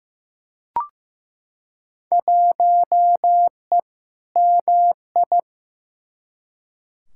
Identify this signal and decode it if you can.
Morse code, heard as a keyed tone, about 15 wpm, 700 Hz. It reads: 1E MI
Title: Field Day Exchanges - 15wpm